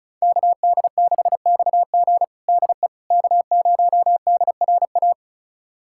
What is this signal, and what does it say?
Morse code, heard as a keyed tone, about 36 wpm, 700 Hz. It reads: KD6XG DE K0DRA
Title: USA Callsign DE Callsign - 5 Characters - 35wpm